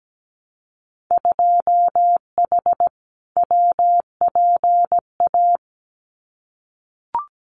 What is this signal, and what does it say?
Morse code, heard as a keyed tone, about 17 wpm, 700 Hz. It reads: 2H WPA
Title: Winter Field Day Exchanges 17wpm